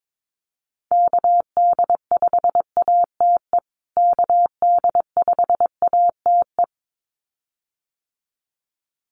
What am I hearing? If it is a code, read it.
Morse code, heard as a keyed tone, about 22 wpm, 700 Hz. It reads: KD5ATE KD5ATE
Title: USA Call Signs Repeated 2x 22wpm